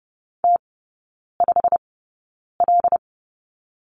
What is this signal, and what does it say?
Morse code, heard as a keyed tone, about 30 wpm, spaced as for 9 wpm, 690 Hz. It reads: T5L
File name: Lesson 16e - Letter c - Three Characters 30wpm